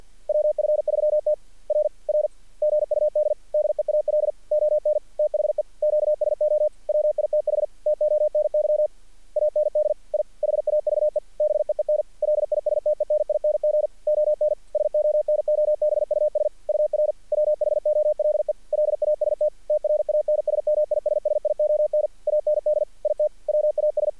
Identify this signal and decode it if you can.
Morse code, heard as a keyed tone, about 48 wpm, 600 Hz. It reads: WF3T R R GUD DEAL ON THE QSO WITH TONY AND I HAVE BEEN LISTENING ON SONOBUS UR WHOLE LAST TRANSMISSION AND IT WAS
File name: archived / CRSnet